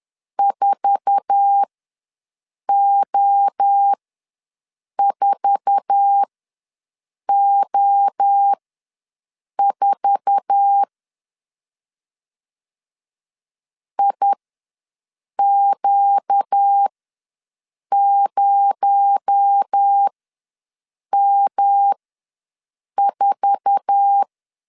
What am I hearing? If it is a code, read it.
Morse code, heard as a keyed tone, about 10 wpm, spaced as for 6 wpm, 800 Hz. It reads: 4O4O4 IQ0M4
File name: LEZIONI / 8 YJ45 / 8 YJ4 ws 10wpm 4020